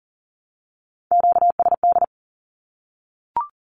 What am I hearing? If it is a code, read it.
Morse code, heard as a keyed tone, about 40 wpm, 700 Hz. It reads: QSD
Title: Q-Codes Repeated 3x 40wpm